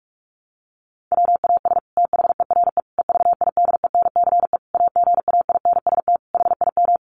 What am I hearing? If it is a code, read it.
Morse code, heard as a keyed tone, about 44 wpm, 700 Hz. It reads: WAS THERE EVIDENCE AGAINST HIM
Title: Sentences from Top 1300 Words 45wpm